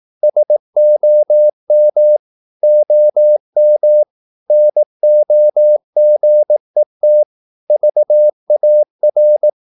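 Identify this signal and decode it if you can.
Morse code, heard as a keyed tone, about 18 wpm, 600 Hz. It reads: SOM OM NOGET VAR
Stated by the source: Slow (QRS) Morse practice at 18 wpm, CW med Gnister.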